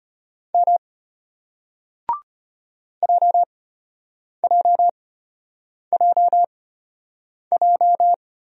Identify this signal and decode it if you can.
Morse code, heard as a keyed tone, about 34 wpm, spaced as for 8 wpm, 700 Hz. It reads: M JJJJ